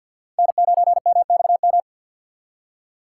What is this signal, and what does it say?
Morse code, heard as a keyed tone, about 64 wpm, 700 Hz. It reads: N9MXM